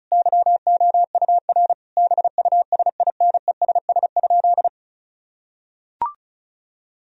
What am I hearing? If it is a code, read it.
Morse code, heard as a keyed tone, about 36 wpm, 700 Hz. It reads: YOUR BUSINESS?
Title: Sentences from Top 1000 Words - Review Part 4 - 35wpm